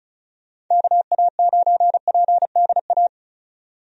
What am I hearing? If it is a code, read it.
Morse code, heard as a keyed tone, about 36 wpm, 700 Hz. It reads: KA9PDA
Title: USA Call Signs - 6 Characters - 35wpm